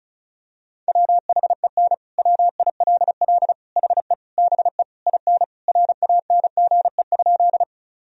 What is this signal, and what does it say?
Morse code, heard as a keyed tone, about 34 wpm, 700 Hz. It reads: WHEN WILL HE BE IN RANGE?
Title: Sentences from Top 1000 Words - Review Part 4 - 35wpm